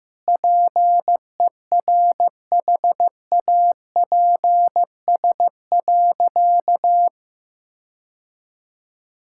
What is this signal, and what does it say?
Morse code, heard as a keyed tone, about 15 wpm, 700 Hz. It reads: PERHAPS.
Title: Sentences from Top 1600 Words 15wpm